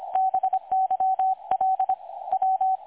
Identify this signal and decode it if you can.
Morse code, heard as a keyed tone, about 25 wpm, 760 Hz. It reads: BYL W
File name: dlazaawansowanych / nagraniazpasma / websdr recording 3565.0kHz